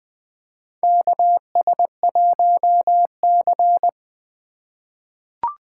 Morse code, heard as a keyed tone, about 20 wpm, 700 Hz. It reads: KS1C
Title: USA Call Signs - 4 Characters - 3x Character Spacing - 20wpm